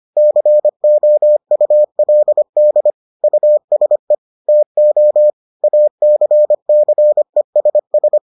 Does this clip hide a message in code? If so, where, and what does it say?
Yes: Morse code from 0.2 to 8.2 s, heard as a keyed tone, about 25 wpm, 600 Hz. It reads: COULD USE TO ACCESS